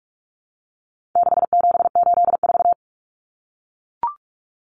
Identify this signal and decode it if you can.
Morse code, heard as a keyed tone, about 44 wpm, 700 Hz. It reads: 6784
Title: Common Cut Numbers - 4-Digit 45wpm